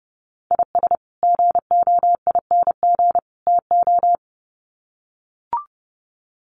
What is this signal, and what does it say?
Morse code, heard as a keyed tone, about 30 wpm, 700 Hz. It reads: IS GOING TO
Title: Sets of 3 Words 30wpm